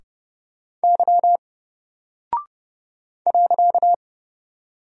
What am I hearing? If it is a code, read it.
Morse code, heard as a keyed tone, about 30 wpm, 700 Hz. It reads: Y .
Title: Single Character (Letter:Number:Punctuation:Prosign) Alt 30wpm